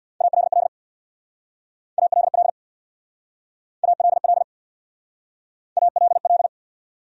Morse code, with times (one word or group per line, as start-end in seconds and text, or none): 0.2-0.7 ALL
2.0-2.5 ALL
3.8-4.4 ALL
5.8-6.5 ALL